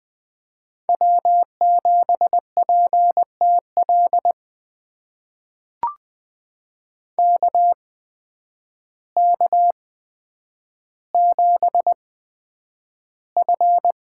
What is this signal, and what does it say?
Morse code, heard as a keyed tone, about 20 wpm, spaced as for 9 wpm, 700 Hz. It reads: W7PTL KK7F
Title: USA Call Signs - 5 Characters - 8x Character Spacing - 20wpm